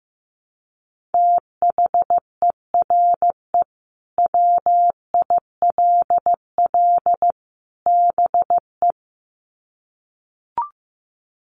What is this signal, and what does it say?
Morse code, heard as a keyed tone, about 15 wpm, 700 Hz. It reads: THERE WILL BE
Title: Sets of 3 Words 15wpm